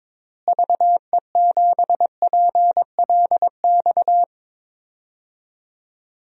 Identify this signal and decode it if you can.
Morse code, heard as a keyed tone, about 23 wpm, 700 Hz. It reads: VE7PLX
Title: DXCC-100 Call Signs - 6 Characters - Repeated 2x - 22wpm